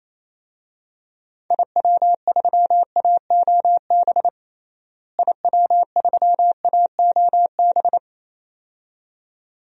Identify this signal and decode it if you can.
Morse code, heard as a keyed tone, about 28 wpm, 700 Hz. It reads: IW3AOB IW3AOB
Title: DXCC-100 Call Signs - 6 Characters - Repeated 2x - 28wpm